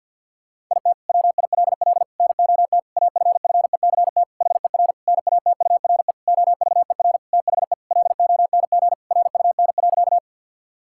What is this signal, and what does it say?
Morse code, heard as a keyed tone, about 50 wpm, 700 Hz. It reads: IT WILL NOT AFFECT HER NATURE OVER THE LONG RUN.
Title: Sentences from Top 1000 Words - Review Part 6 - 50wpm